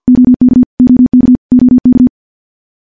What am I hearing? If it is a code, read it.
Morse code, heard as a keyed tone, about 50 wpm, 250 Hz. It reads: OK OK OK
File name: morse / morse-training-50WPM